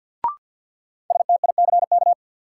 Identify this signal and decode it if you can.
Morse code, heard as a keyed tone, about 53 wpm, 700 Hz. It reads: STICK
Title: Sentences from Top 1200 Words - New Word Drill Alt - 50wpm